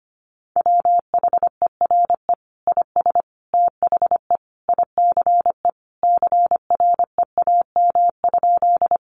Morse code, heard as a keyed tone, about 25 wpm, 700 Hz. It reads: WHERE IS THE ICE CREAM?
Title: Sentences from Top 1500 Words 25wpm